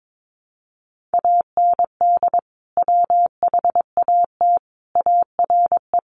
Morse code, heard as a keyed tone, about 22 wpm, 700 Hz. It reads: AND WHAT ARE
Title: Sentences from Top 1200 Words 22wpm